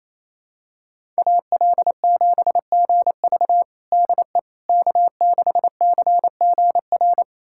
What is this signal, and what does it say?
Morse code, heard as a keyed tone, about 27 wpm, 700 Hz. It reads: AL7GV DE K6CGR
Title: USA Callsign DE Callsign - 5 Characters - 28wpm